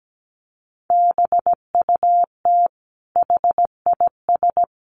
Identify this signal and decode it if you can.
Morse code, heard as a keyed tone, about 17 wpm, 700 Hz. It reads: BUT HIS
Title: Sentences from Top 1500 Words 17wpm